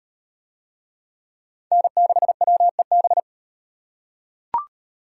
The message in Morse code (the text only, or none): N6WEB